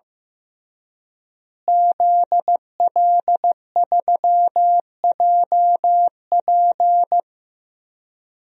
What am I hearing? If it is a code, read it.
Morse code, heard as a keyed tone, about 15 wpm, 700 Hz. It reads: ZL3JP
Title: DXCC-100 Call Signs - 5 Characters - Repeated 2x - 15wpm